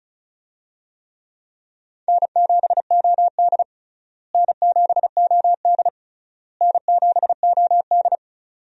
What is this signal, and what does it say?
Morse code, heard as a keyed tone, about 36 wpm, 700 Hz. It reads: N7OD N7OD N7OD
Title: USA Call Signs - 4 Characters - Repeated 3x - 35wpm